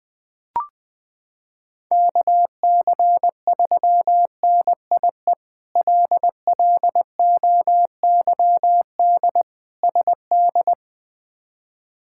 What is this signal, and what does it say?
Morse code, heard as a keyed tone, about 21 wpm, 700 Hz. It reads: KC3NIE LLOYD SD
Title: K1USN SST QSO Practice - 20wpm